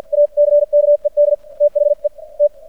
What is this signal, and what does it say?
Morse code, heard as a keyed tone, about 51 wpm, spaced as for 44 wpm, 590 Hz. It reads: ACKED THE T
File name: archived / CRSnet